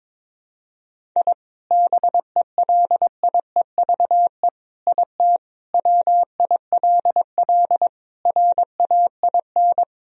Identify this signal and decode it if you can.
Morse code, heard as a keyed tone, about 22 wpm, 700 Hz. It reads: I BELIEVE IT WILL RAIN
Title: Sentences from Top 1000 Words - Review Part 1 - 22wpm